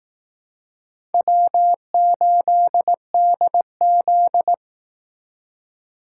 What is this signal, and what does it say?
Morse code, heard as a keyed tone, about 19 wpm, 700 Hz. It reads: W8DZ
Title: Speed Racing - USA 1x2 Call Signs 15wpm